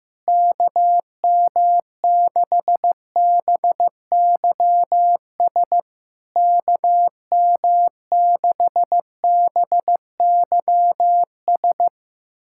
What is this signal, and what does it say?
Morse code, heard as a keyed tone, about 15 wpm, 700 Hz. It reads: KM6BYS KM6BYS
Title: SOTA QSO Practice 15wpm